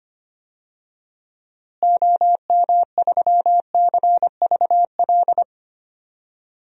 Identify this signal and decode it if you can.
Morse code, heard as a keyed tone, about 25 wpm, 700 Hz. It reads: OM3CVL
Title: DXCC-100 Call Signs - 6 Characters - Repeated 3x - 25wpm